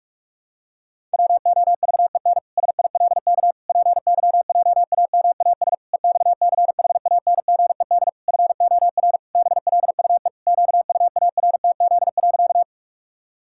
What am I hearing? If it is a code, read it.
Morse code, heard as a keyed tone, about 41 wpm, 700 Hz. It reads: WOVEN SILK PYJAMAS EXCHANGED FOR BLUE QUARTZ.
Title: Pangram Challenge v2 - 45wpm